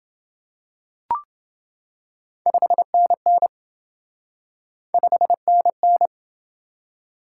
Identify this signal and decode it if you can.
Morse code, heard as a keyed tone, about 29 wpm, 700 Hz. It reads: 5NN 5NN